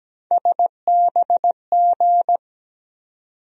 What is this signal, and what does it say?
Morse code, heard as a keyed tone, about 17 wpm, 700 Hz. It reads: SBG
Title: USA Call Sign Postfixes - 3 Letters 17wpm